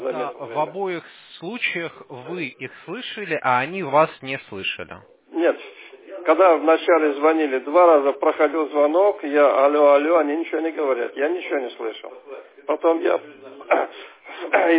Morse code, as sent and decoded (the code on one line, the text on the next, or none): none